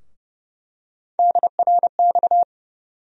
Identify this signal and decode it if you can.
Morse code, heard as a keyed tone, about 31 wpm, 700 Hz. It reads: DRX
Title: USA Call Sign Postfixes - 3 Letters 30wpm